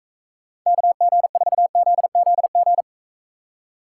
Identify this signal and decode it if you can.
Morse code, heard as a keyed tone, about 43 wpm, 700 Hz. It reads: KG4ZZG